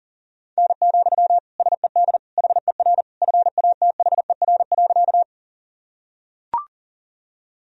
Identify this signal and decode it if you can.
Morse code, heard as a keyed tone, about 40 wpm, 700 Hz. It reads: N, SED HER FATHER.
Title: Sentences from Top 300 Words and Abbreviations 40wpm